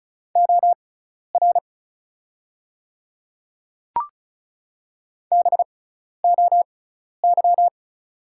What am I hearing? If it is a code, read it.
Morse code, heard as a keyed tone, about 36 wpm, spaced as for 12 wpm, 700 Hz. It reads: OR BOY
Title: 6x Character Spacing - Top 200 Words - 35wpm